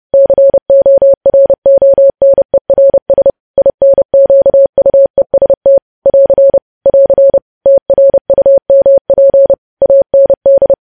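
Morse code, heard as a keyed tone, about 30 wpm, 550 Hz. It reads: CORONERS INQUEST <AR> <AR> TRUMP AND
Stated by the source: News headlines sent in Morse code at 30.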